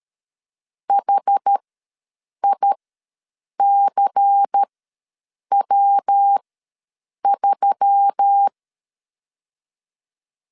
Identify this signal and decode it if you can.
Morse code, heard as a keyed tone, about 13 wpm, spaced as for 7 wpm, 800 Hz. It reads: HICW3
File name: LEZIONI / 5 LBVC / 5 LBVC ws 12wpm 3159